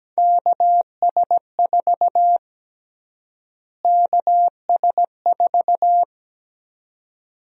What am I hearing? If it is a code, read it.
Morse code, heard as a keyed tone, about 17 wpm, 700 Hz. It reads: KS4 KS4
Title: USA Call Sign Prefixes - Repeated 2x - 17wpm